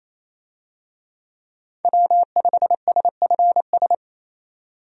Morse code, heard as a keyed tone, about 28 wpm, 700 Hz. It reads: W5SFS
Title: USA Call Signs - 5 Characters - 28wpm